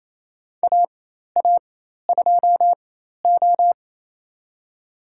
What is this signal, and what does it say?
Morse code, heard as a keyed tone, about 28 wpm, spaced as for 13 wpm, 700 Hz. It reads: AA2O
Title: USA Call Signs - 4 Characters - 4x Character Spacing - 28wpm